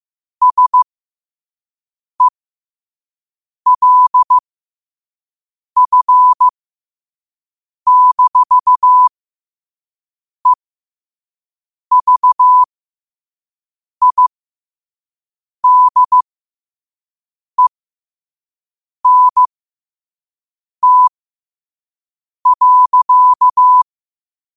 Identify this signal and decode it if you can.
Morse code, heard as a keyed tone, about 15 wpm, spaced as for 5 wpm, 1000 Hz. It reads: SELF-EVIDENT.
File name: qotdfiles / 5wpm